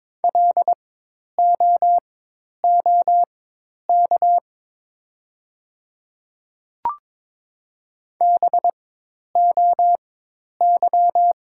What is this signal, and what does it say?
Morse code, heard as a keyed tone, about 22 wpm, spaced as for 10 wpm, 700 Hz. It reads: LOOK BOY
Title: 4x Character Spacing - Top 200 Words - 22wpm